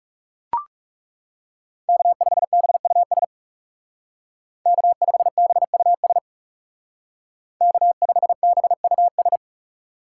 Morse code, heard as a keyed tone, about 39 wpm, 700 Hz. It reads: K5BUS K5BUS K5BUS